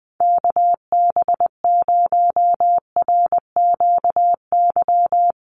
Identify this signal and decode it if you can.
Morse code, heard as a keyed tone, about 20 wpm, 700 Hz. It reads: KB0RQY
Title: USA Call Signs 20wpm